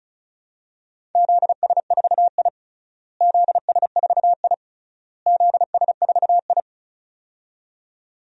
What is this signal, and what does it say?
Morse code, heard as a keyed tone, about 36 wpm, 700 Hz. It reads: ZS4I ZS4I ZS4I
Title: DXCC-100 Call Signs - 4 Characters - Repeated 3x - 35wpm